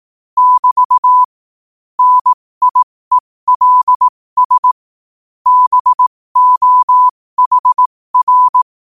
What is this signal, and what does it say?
Morse code, heard as a keyed tone, about 18 wpm, spaced as for 15 wpm, 1000 Hz. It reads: <BT> NIELS BOHR
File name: qotdfiles / 15wpm